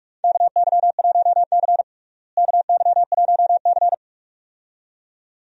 Morse code, heard as a keyed tone, about 46 wpm, 700 Hz. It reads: KY1C KY1C
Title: USA Call Signs - 4 Characters - Repeated 2x - 45wpm